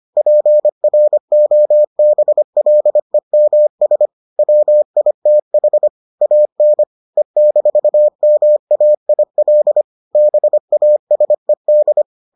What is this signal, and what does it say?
Morse code, heard as a keyed tone, about 25 wpm, 600 Hz. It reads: PROBLEMS WITH AN E-MAIL BASED